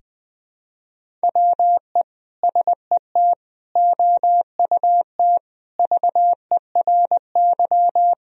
Morse code, heard as a keyed tone, about 20 wpm, 700 Hz. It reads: WE SET OUT VERY
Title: Sentences from Top 1000 Words - Review Part 2 - 20wpm